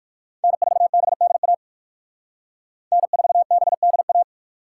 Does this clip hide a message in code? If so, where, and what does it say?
Yes: Morse code from 0.4 to 4.2 s, heard as a keyed tone, about 66 wpm, spaced as for 57 wpm, 700 Hz. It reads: N4BDA N4BDA